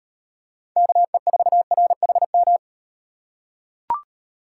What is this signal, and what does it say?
Morse code, heard as a keyed tone, about 35 wpm, 700 Hz. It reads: KE4RHM